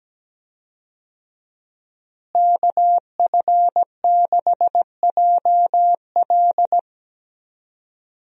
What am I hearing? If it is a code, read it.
Morse code, heard as a keyed tone, about 17 wpm, 700 Hz. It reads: KF6JL
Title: USA Call Signs - 5 Characters - Repeated 3x - 17wpm